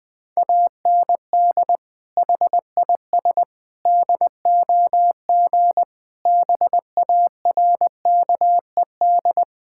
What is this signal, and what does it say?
Morse code, heard as a keyed tone, about 21 wpm, 700 Hz. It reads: AND HIS DOG BARKED
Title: Sentences with 2-Word Phrases - 20wpm